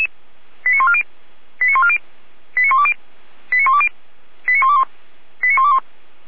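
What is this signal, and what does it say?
Five-tone selective call (ZVEI), heard as quick runs of five tones, tones of about 70 ms. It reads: 89140 89140 89120 89120 89121 89121
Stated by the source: Five-tone ZVEI selective call 89119 received on an SDR.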